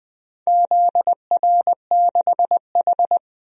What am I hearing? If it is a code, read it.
Morse code, heard as a keyed tone, about 20 wpm, 700 Hz. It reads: ZR6H
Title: DXCC-100 Call Signs - 4 Characters - 20wpm